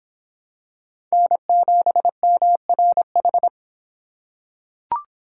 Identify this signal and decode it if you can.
Morse code, heard as a keyed tone, about 26 wpm, 700 Hz. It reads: N7MRH